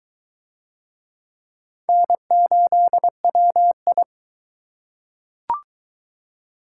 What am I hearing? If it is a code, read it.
Morse code, heard as a keyed tone, about 21 wpm, 700 Hz. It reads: N8WI